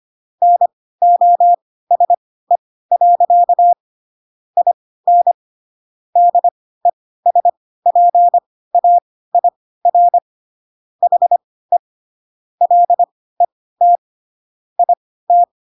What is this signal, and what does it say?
Morse code, heard as a keyed tone, about 25 wpm, spaced as for 16 wpm, 700 Hz. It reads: NOSE. IN DESPAIR HE LET IT
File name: Pinocchio / Pinocchio ch3 10 25 ik1hge